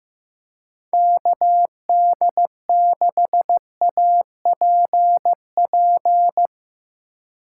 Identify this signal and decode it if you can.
Morse code, heard as a keyed tone, about 15 wpm, 700 Hz. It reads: KD6APP
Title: USA Call Signs - 6 Characters - 2x Character Spacing - 15wpm